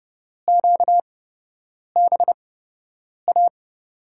Morse code, heard as a keyed tone, about 32 wpm, spaced as for 8 wpm, 700 Hz. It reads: QBA
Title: USA Call Signs - 8x Character Spacing - 30wpm